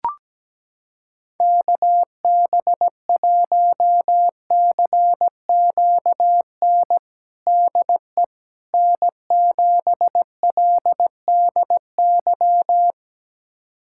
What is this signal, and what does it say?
Morse code, heard as a keyed tone, about 17 wpm, 700 Hz. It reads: KB1CQN DE N7LDY
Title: USA Callsign DE Callsign 17wpm